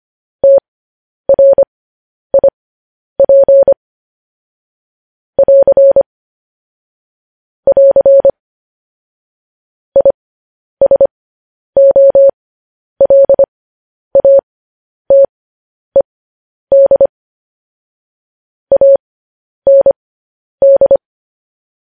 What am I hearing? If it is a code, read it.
Morse code, heard as a keyed tone, about 25 wpm, spaced as for 10 wpm, 550 Hz. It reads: TRIP <AR> <AR> ISOLATED AND
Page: news-headlines-in-morse-code-at-10-wpm